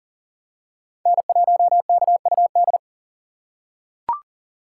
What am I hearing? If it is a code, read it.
Morse code, heard as a keyed tone, about 40 wpm, 700 Hz. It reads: N1KUD
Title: USA Call Signs - 5 Characters - 2x Character Spacing - 40wpm